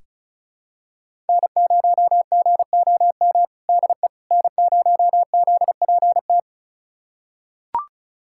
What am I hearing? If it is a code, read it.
Morse code, heard as a keyed tone, about 36 wpm, 700 Hz. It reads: N0GOM DE N0ZPT
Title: USA Callsign DE Callsign - 5 Characters - 35wpm